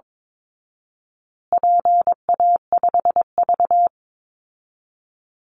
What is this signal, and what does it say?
Morse code, heard as a keyed tone, about 22 wpm, 700 Hz. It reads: PA5V
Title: Call Signs Repeated 3x - 22wpm